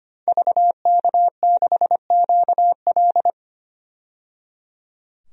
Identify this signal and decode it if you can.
Morse code, heard as a keyed tone, about 25 wpm, 700 Hz. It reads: VK6QL
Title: DXCC-100 Call Signs - 5 Characters - 25wpm